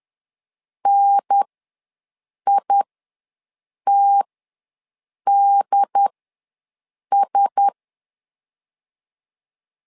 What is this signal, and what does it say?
Morse code, heard as a keyed tone, about 10 wpm, spaced as for 6 wpm, 800 Hz. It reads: NITDS
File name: LEZIONI / 4 Q1S3 ws 10wpm 2437